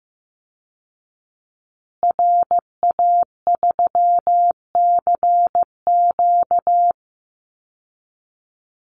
Morse code, heard as a keyed tone, about 15 wpm, 700 Hz. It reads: RA3CQ
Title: Call Signs Repeated 3x - 15wpm